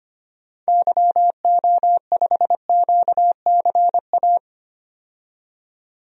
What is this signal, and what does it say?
Morse code, heard as a keyed tone, about 25 wpm, 700 Hz. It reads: YO5QCA
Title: DXCC-100 Call Signs - 6 Characters - 25wpm